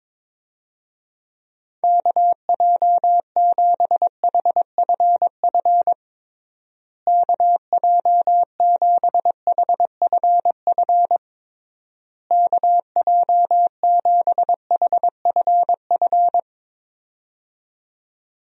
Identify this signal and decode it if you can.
Morse code, heard as a keyed tone, about 22 wpm, 700 Hz. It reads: KJ7HFF KJ7HFF KJ7HFF
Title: USA Call Signs - 6 Characters - Repeated 3x - 22wpm